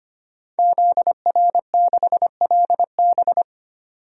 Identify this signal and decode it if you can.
Morse code, heard as a keyed tone, about 25 wpm, 700 Hz. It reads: ZR6LB
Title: DXCC-100 Call Signs - 5 Characters - 25wpm